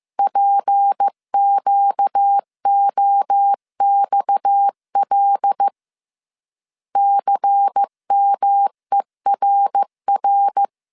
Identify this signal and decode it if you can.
Morse code, heard as a keyed tone, about 15 wpm, 800 Hz. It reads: PQOXL CMERR